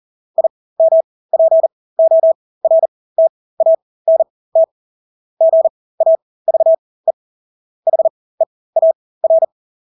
Morse code, heard as a keyed tone, about 40 wpm, spaced as for 20 wpm, 650 Hz. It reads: IMPORTANT GAVE HEAR
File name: Engelske ord 40-20 wpm